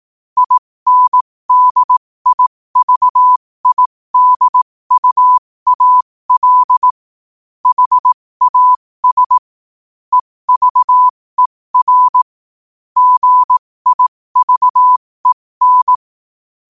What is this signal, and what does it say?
Morse code, heard as a keyed tone, about 18 wpm, spaced as for 15 wpm, 1000 Hz. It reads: INDIVIDUAL HAS EVER GIVEN
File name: qotdfiles / 15wpm